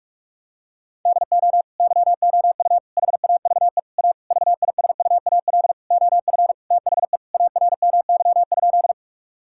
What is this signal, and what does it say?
Morse code, heard as a keyed tone, about 41 wpm, 690 Hz. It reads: DO YOU HAVE A VISUAL OF THE ARMY?
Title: Sentences from Top 1000 Words - Review Part 6 - 45wpm